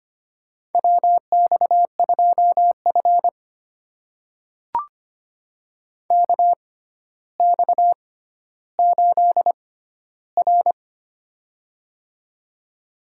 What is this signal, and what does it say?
Morse code, heard as a keyed tone, about 25 wpm, spaced as for 13 wpm, 700 Hz. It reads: WX2F KX8R
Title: USA Call Signs - 4 Characters - 6x Character Spacing - 25wpm